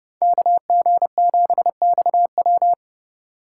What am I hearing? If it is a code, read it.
Morse code, heard as a keyed tone, about 31 wpm, 700 Hz. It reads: KG7XW
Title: USA Call Signs - 5 Characters - 3x Character Spacing - 30wpm